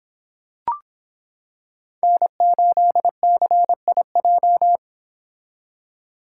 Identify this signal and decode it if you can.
Morse code, heard as a keyed tone, about 26 wpm, 700 Hz. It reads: N8CIJ